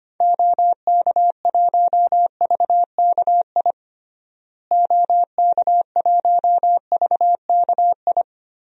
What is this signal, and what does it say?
Morse code, heard as a keyed tone, about 25 wpm, 700 Hz. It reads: OK1VKI OK1VKI
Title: DXCC-100 Call Signs - 6 Characters - Repeated 2x - 25wpm